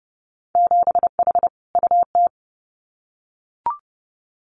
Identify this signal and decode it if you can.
Morse code, heard as a keyed tone, about 30 wpm, 700 Hz. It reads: 7H UT
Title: Winter Field Day Exchanges Encore 30wpm